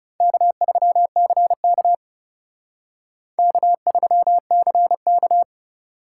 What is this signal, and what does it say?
Morse code, heard as a keyed tone, about 34 wpm, 700 Hz. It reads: K3CK K3CK
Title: Speed Racing - USA 1x2 Call Signs 30wpm